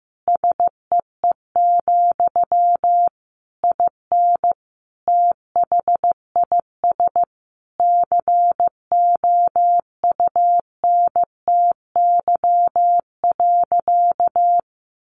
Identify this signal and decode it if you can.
Morse code, heard as a keyed tone, about 15 wpm, 700 Hz. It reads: SEE, IN THIS COUNTY.
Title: Sentences from Top 1200 Words 15wpm